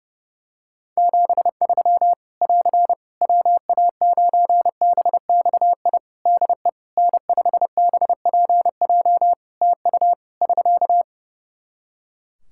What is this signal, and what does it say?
Morse code, heard as a keyed tone, about 30 wpm, 700 Hz. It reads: 73 <AR> WA9BXI DE N5BPJ TU <SK>
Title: K1USN SST QSO Practice 30wpm